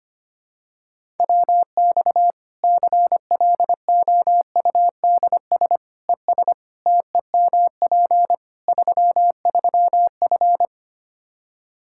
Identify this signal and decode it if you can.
Morse code, heard as a keyed tone, about 25 wpm, 700 Hz. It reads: WX CLOUDS ES TEMP 33F
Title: K1USN SST QSO Practice 25wpm